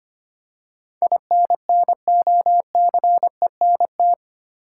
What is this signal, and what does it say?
Morse code, heard as a keyed tone, about 25 wpm, 700 Hz. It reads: INNOCENT
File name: 8 Letter Words - First 3 Letters Given - 25wpm